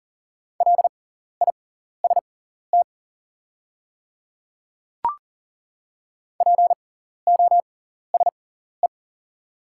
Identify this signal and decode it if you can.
Morse code, heard as a keyed tone, about 40 wpm, spaced as for 14 wpm, 700 Hz. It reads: LIST POSE
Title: 6x Character Spacing - Top 500 Words - 40wpm